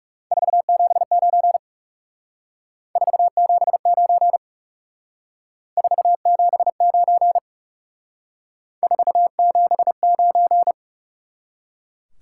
Morse code, heard as a keyed tone, about 38 wpm, 700 Hz. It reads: 479 479 479 479